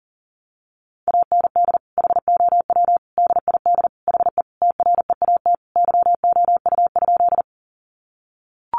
Morse code, heard as a keyed tone, about 40 wpm, 700 Hz. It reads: AND HOW DID HE TREAT YOU?
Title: Sentences from Top 1300 Words 40wpm